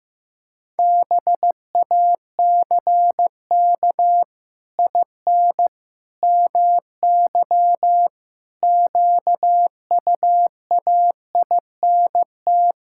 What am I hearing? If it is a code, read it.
Morse code, heard as a keyed tone, about 15 wpm, 700 Hz. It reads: BACK IN MY QUAINT
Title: Pangram Challenge v2 - 15wpm